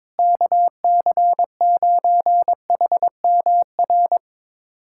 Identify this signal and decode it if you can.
Morse code, heard as a keyed tone, about 22 wpm, 700 Hz. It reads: KC9HMR
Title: USA Callsign DE Callsign 22wpm